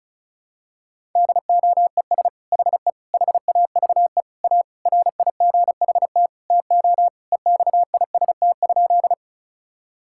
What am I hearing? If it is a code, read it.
Morse code, heard as a keyed tone, about 34 wpm, 700 Hz. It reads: DOES HE HAVE A RIGHT TO EXIST?
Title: Sentences from Top 1000 Words - Review Part 5 - 35wpm